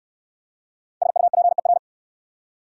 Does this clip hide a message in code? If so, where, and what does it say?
Yes: Morse code from 1.0 to 1.8 s, heard as a keyed tone, about 58 wpm, 690 Hz. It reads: SUPER